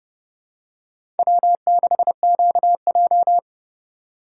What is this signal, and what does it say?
Morse code, heard as a keyed tone, about 32 wpm, 700 Hz. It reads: W6QJ
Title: USA Call Signs - 4x Character Spacing - 30wpm